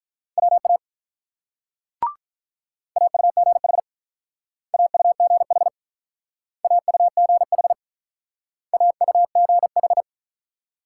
Morse code, heard as a keyed tone, about 48 wpm, 700 Hz. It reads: WA AUGH AUGH AUGH AUGH